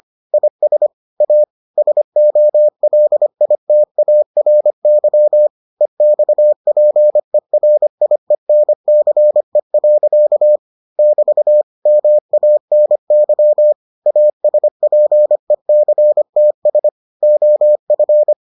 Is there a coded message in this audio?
Yes: Morse code from 0.3 to 18.3 s, heard as a keyed tone, about 25 wpm, 600 Hz. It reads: IS A SOLITARY EXPERIENCE. <BT> MANY ASPECTS OF